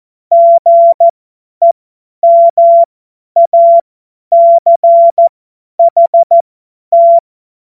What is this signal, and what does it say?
Morse code, heard as a keyed tone, about 14 wpm, spaced as for 10 wpm, 680 Hz. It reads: GEMACHT
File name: Morsekurs / Witze / Witz 23. Mai 2023 50 F70